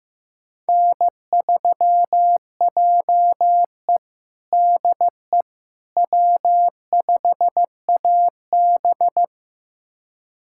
Morse code, heard as a keyed tone, about 15 wpm, 700 Hz. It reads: N3JE DE W5AB
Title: USA Callsign DE Callsign - 4 Characters - 15wpm